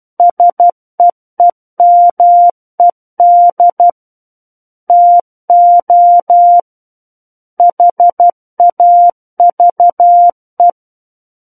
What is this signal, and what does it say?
Morse code, heard as a keyed tone, about 12 wpm, 700 Hz. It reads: SEEMED TO HAVE
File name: Book One / 03-Chapter 3 - 12 WPM